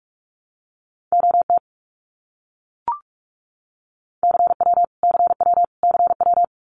Morse code, heard as a keyed tone, about 44 wpm, 700 Hz. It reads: OT CW CW CW
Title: Abbreviations Repeated 3x - 45wpm